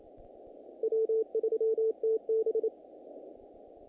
Morse code, heard as a keyed tone, about 26 wpm, 440 Hz. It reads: W3TB
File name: contests / naqp-cw-k5tr-op-k5pi- -2000